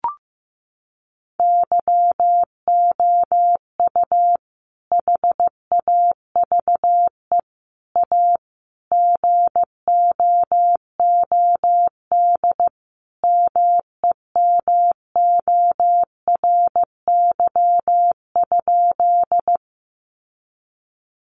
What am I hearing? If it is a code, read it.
Morse code, heard as a keyed tone, about 14 wpm, 700 Hz. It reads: YOU HAVE A GOOD MEMORY?
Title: Sentences from Top 1300 Words 15wpm